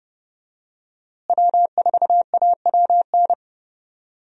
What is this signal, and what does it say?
Morse code, heard as a keyed tone, about 31 wpm, 700 Hz. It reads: W4AWN
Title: USA Call Signs - 5 Characters - Repeated 3x - 30wpm